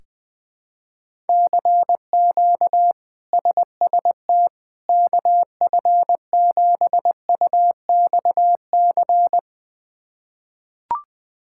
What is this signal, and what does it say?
Morse code, heard as a keyed tone, about 20 wpm, 700 Hz. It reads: CQ SST KF7UXC
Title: K1USN SST QSO Practice - 20wpm